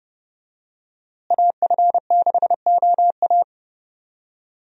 Morse code, heard as a keyed tone, about 29 wpm, 700 Hz. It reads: AF6OA
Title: USA Call Signs - 5 Characters - Repeated 2x - 30wpm